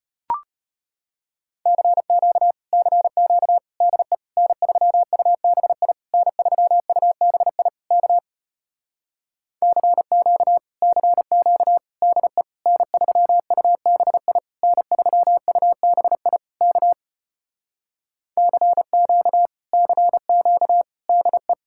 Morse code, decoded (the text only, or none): CQ CQ DE N3UBI N3UBI K CQ CQ DE N3UBI N3UBI K CQ CQ DE